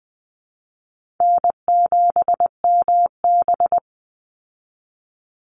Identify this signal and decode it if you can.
Morse code, heard as a keyed tone, about 20 wpm, 700 Hz. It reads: N7MB
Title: Easy Call Signs Repeated 2x - 20wpm